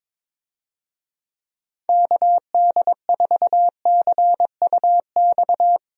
Morse code, heard as a keyed tone, about 21 wpm, 700 Hz. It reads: KD4CUX
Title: USA Call Signs - 6 Characters - Repeated 2x - 22wpm